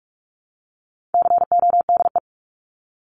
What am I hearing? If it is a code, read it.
Morse code, heard as a keyed tone, about 44 wpm, 700 Hz. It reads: CODE
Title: Top 1501-1750 Words 45wpm